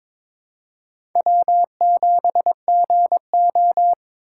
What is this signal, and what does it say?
Morse code, heard as a keyed tone, about 22 wpm, 700 Hz. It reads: W7GO
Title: USA Call Signs - 4 Characters - 3x Character Spacing - 22wpm